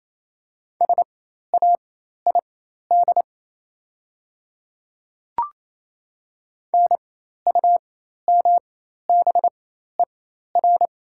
Morse code, heard as a keyed tone, about 28 wpm, spaced as for 13 wpm, 700 Hz. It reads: SAID NUMBER
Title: 4x Character Spacing - Top 100 Words - 28wpm